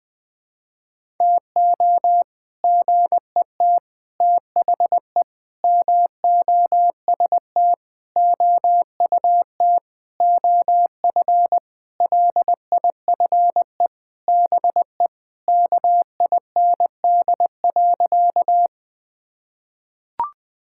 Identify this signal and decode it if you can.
Morse code, heard as a keyed tone, about 19 wpm, 700 Hz. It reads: TO GET THE MOST OUT OF LIFE BE KIND.
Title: Sentences with 6-Word Phrases - 20wpm